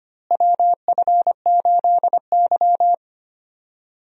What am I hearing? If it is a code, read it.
Morse code, heard as a keyed tone, about 25 wpm, 700 Hz. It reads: WF8Y
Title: USA Call Signs - 4 Characters - Repeated 2x - 25wpm